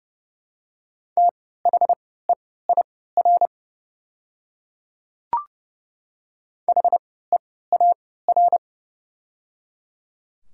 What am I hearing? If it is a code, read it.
Morse code, heard as a keyed tone, about 30 wpm, spaced as for 17 wpm, 700 Hz. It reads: THEIR HEAR
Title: 3x Character Spacing - Top 300 Words - 30wpm